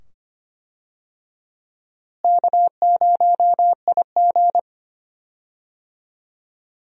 Morse code, heard as a keyed tone, about 25 wpm, 700 Hz. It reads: K0IG
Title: USA Call Signs - 4 Characters - Repeated 2x - 25wpm